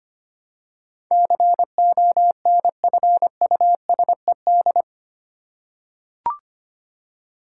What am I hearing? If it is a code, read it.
Morse code, heard as a keyed tone, about 25 wpm, 700 Hz. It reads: CONFUSED
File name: Eight Letter Words - 25wpm